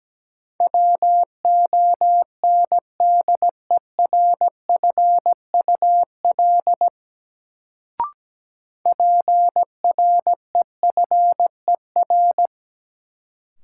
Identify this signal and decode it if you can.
Morse code, heard as a keyed tone, about 17 wpm, 700 Hz. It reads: WONDERFUL PREFER
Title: Sentences from Top 1100 Words - New Word Drill - 17wpm